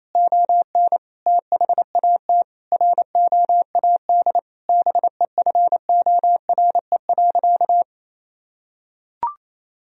Morse code, heard as a keyed tone, about 27 wpm, 700 Hz. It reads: ON THAT ROAD BEFORE.
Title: Sentences from Top 1000 Words - Review Part 2 - 28wpm